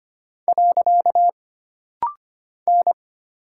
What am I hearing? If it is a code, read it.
Morse code, heard as a keyed tone, about 25 wpm, 700 Hz. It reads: . N
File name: Single Character (Letter:Number:Punctuation:Prosign) - Rapid-Fire - 25wpm